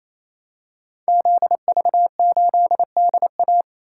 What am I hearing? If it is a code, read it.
Morse code, heard as a keyed tone, about 29 wpm, 700 Hz. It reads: ZV8DA
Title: Code Groups - 5 Letter:Numbers - 28wpm